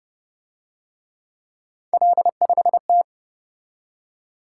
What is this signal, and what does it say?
Morse code, heard as a keyed tone, about 30 wpm, 700 Hz. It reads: L5T